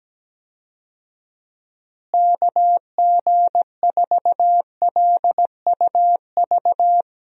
Morse code, heard as a keyed tone, about 17 wpm, 700 Hz. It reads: KG4LUV
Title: USA Call Signs - 6 Characters - Repeated 3x - 17wpm